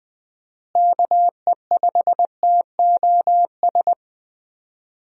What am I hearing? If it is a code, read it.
Morse code, heard as a keyed tone, about 20 wpm, 700 Hz. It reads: KE5TOS
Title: USA Call Signs - 6 Characters - Repeated 2x - 20wpm